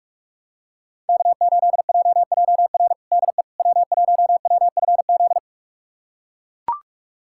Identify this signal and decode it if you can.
Morse code, heard as a keyed tone, about 41 wpm, 700 Hz. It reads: K8JJR DE W1WFZ
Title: USA Callsign DE Callsign - 5 Characters - 45wpm